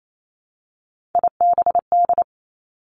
Morse code, heard as a keyed tone, about 28 wpm, 700 Hz. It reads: IBD
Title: Three Challenging Letter-Number Combinations 28wpm